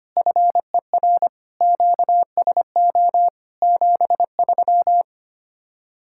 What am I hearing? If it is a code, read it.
Morse code, heard as a keyed tone, about 31 wpm, 700 Hz. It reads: FER QSO 73
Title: Speed-Racing Common QSO Phrases - 25wpm